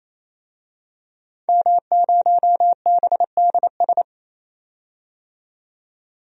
Morse code, heard as a keyed tone, about 28 wpm, 700 Hz. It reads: M0BDS
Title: DXCC-100 Call Signs - 5 Characters - 28wpm